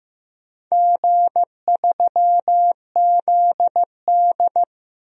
Morse code, heard as a keyed tone, about 15 wpm, 700 Hz. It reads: G3ZD
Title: DXCC-100 Call Signs - 4 Characters - Repeated 3x - 15wpm